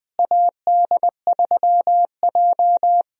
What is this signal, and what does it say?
Morse code, heard as a keyed tone, about 20 wpm, 700 Hz. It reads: AD3J
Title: USA Call Signs - 4 Characters - Repeated 3x - 20wpm